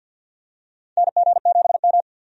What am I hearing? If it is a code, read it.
Morse code, heard as a keyed tone, about 49 wpm, 700 Hz. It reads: NG7M
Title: USA Call Signs - 4 Characters - 50wpm